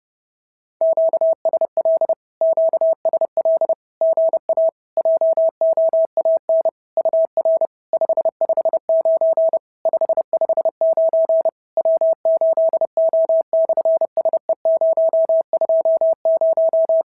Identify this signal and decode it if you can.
Morse code, heard as a keyed tone, about 30 wpm, 650 Hz. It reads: QSL QSL GA JOAN UR 559 559 W8O/SE020
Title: SOTA QSO Practice 30wpm